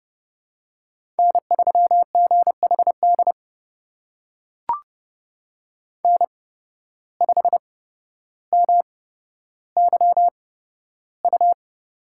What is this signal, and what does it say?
Morse code, heard as a keyed tone, about 32 wpm, spaced as for 13 wpm, 700 Hz. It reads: N3GHD N5MYU
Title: USA Call Signs - 5 Characters - 8x Character Spacing - 30wpm